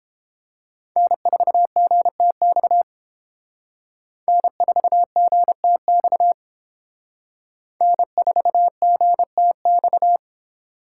Morse code, heard as a keyed tone, about 29 wpm, 700 Hz. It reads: N4GTX N4GTX N4GTX